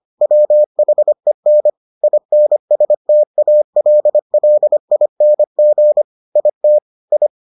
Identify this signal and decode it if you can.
Morse code, heard as a keyed tone, about 25 wpm, 600 Hz. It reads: WHEN INSTALLING IT I